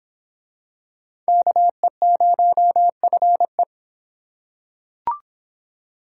Morse code, heard as a keyed tone, about 23 wpm, 700 Hz. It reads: KE0FE